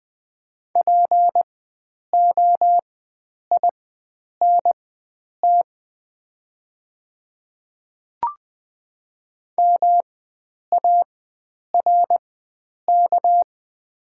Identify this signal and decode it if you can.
Morse code, heard as a keyed tone, about 20 wpm, spaced as for 9 wpm, 700 Hz. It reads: POINT MARK
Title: 4x Character Spacing - Top 500 Words - 20wpm